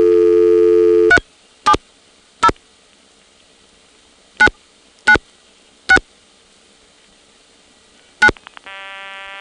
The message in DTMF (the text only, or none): 370 996 9